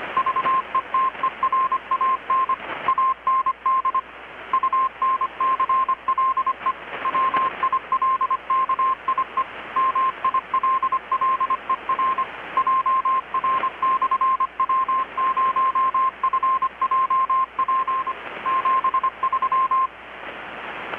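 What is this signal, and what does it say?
Morse code, heard as a keyed tone, about 25 wpm, 1050 Hz, with radio noise. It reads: VETERAN AND UNCLE WILKIE MILLER JR/W0FJP 73
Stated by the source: Morse from KPH's Night of Nights XX maritime radiotelegraphy event, received in Chiapas, Mexico about 3,800 km away on a Kenwood R-600 with a grounded 10 m long-wire antenna.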